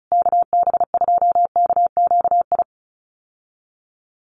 Morse code, heard as a keyed tone, about 34 wpm, 700 Hz. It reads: KB2KQI
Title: USA Call Signs Repeated 3x 35wpm